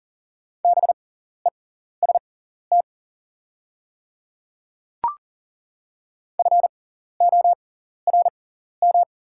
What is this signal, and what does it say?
Morse code, heard as a keyed tone, about 40 wpm, spaced as for 14 wpm, 690 Hz. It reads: BEST FORM
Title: 6x Character Spacing - Top 500 Words - 40wpm